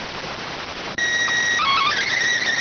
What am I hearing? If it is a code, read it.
SSTV (slow-scan television), PD 290.